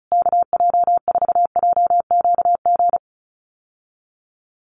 Morse code, heard as a keyed tone, about 34 wpm, 700 Hz. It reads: KJ4JQG
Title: USA Call Signs Repeated 3x 35wpm